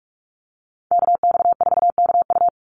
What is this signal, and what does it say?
Morse code, heard as a keyed tone, about 44 wpm, 700 Hz. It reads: KX4KU
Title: Call Signs - 45wpm